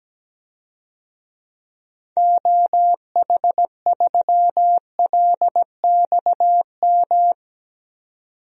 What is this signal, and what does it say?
Morse code, heard as a keyed tone, about 17 wpm, 700 Hz. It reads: OH3LXM
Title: DXCC-100 Call Signs - 6 Characters - 17wpm